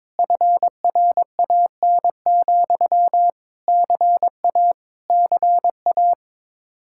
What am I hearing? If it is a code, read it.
Morse code, heard as a keyed tone, about 22 wpm, 700 Hz. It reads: FRAN, CA CA
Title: Speed-Racing Common QSO Phrases - 22wpm